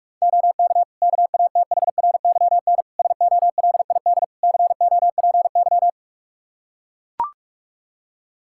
Morse code, heard as a keyed tone, about 41 wpm, 700 Hz. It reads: OK KATHRYN SOLID COPY
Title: K1USN SST QSO Practice 45wpm